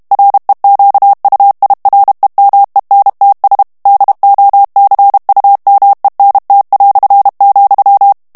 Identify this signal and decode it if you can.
Morse code, heard as a keyed tone, about 32 wpm, 800 Hz. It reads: REQUIREMENTS DOCUMENT",
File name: morse mp3 / ESPAGNOL TEXTE EN CLAIR 1 VITESSE 30 MOTS MINUTES